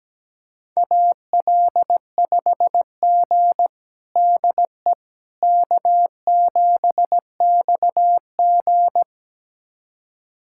Morse code, heard as a keyed tone, about 17 wpm, 700 Hz. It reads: AL5G DE K7XG
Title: USA Callsign DE Callsign - 4 Characters - 17wpm